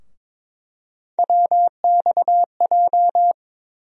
Morse code, heard as a keyed tone, about 21 wpm, 700 Hz. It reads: WXJ
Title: USA Call Sign Postfixes - 3 Letters 22wpm